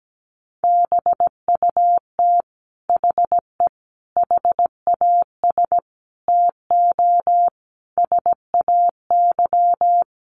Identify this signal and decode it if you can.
Morse code, heard as a keyed tone, about 17 wpm, 700 Hz. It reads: BUT HE HAS TO SAY
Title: Sentences from Top 600 Words 17wpm